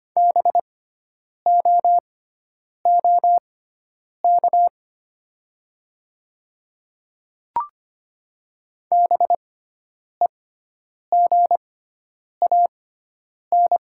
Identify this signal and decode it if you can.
Morse code, heard as a keyed tone, about 25 wpm, spaced as for 9 wpm, 700 Hz. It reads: BOOK BEGAN
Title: 6x Character Spacing - Top 300 Words - 25wpm